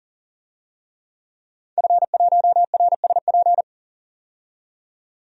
Morse code, heard as a keyed tone, about 40 wpm, 700 Hz. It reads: F1RSP
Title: DXCC-100 Call Signs - 5 Characters - 40wpm